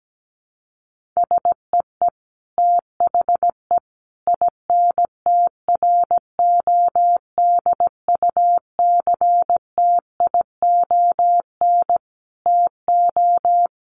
Morse code, heard as a keyed tone, about 17 wpm, 700 Hz. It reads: SEE THE INTRODUCTION TO
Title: Sentences from Top 1300 Words 17wpm